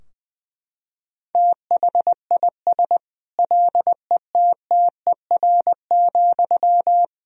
Morse code, heard as a keyed tone, about 20 wpm, 700 Hz. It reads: THIS LETTER,
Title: Sentences from Top 1000 Words - Review Part 4 - 20wpm